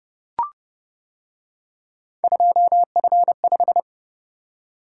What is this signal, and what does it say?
Morse code, heard as a keyed tone, about 28 wpm, 700 Hz. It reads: 2F5